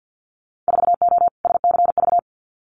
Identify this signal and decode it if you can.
Morse code, heard as a keyed tone, about 50 wpm, 700 Hz. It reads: 4O SCV
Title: Winter Field Day Exchanges 50wpm